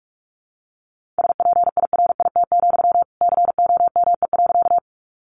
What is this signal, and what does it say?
Morse code, heard as a keyed tone, about 44 wpm, 700 Hz. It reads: SPIRIT, COME.
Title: Sentences from Top 1400 Words 45wpm